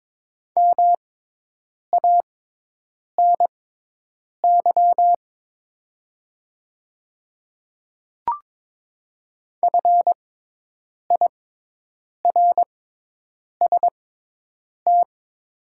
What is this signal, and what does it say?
Morse code, heard as a keyed tone, about 22 wpm, spaced as for 8 wpm, 700 Hz. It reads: MANY FIRST
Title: 6x Character Spacing - Top 100 Words - 22wpm